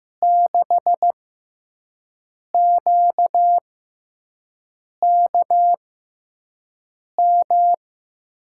Morse code, heard as a keyed tone, about 15 wpm, spaced as for 5 wpm, 700 Hz. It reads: 6QKM
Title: USA Call Signs - 6x Character Spacing - 15wpm